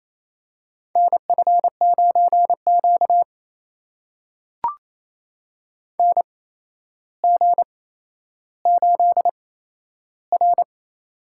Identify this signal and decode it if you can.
Morse code, heard as a keyed tone, about 29 wpm, spaced as for 12 wpm, 700 Hz. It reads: NF9Q NG8R
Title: USA Call Signs - 4 Characters - 8x Character Spacing - 28wpm